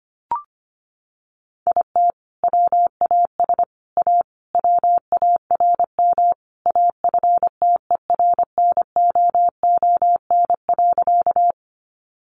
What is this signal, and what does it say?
Morse code, heard as a keyed tone, about 25 wpm, 700 Hz. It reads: IT WAS A WARM AFTERNOON.
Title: Sentences from Top 1500 Words 25wpm